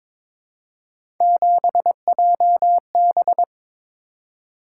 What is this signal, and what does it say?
Morse code, heard as a keyed tone, about 26 wpm, 700 Hz. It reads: 7JB
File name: Speed-Racing - Three Letter-Number Combinations - 22wpm